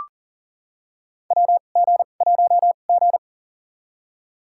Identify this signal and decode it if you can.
Morse code, heard as a keyed tone, about 40 wpm, spaced as for 29 wpm, 700 Hz. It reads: WG1G
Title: USA Call Signs - 2x Character Spacing - 40wpm